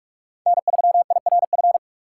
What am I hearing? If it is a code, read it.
Morse code, heard as a keyed tone, about 44 wpm, 700 Hz. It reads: N3IRF
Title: USA Call Signs - 6x Character Spacing - 45wpm